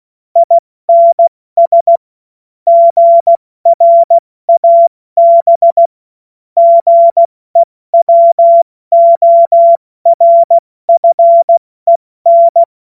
Morse code, heard as a keyed tone, about 16 wpm, 680 Hz. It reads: INS GRAB GEWORFEN
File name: Morsekurs / Witze / Witz 06. September 2022 70 F80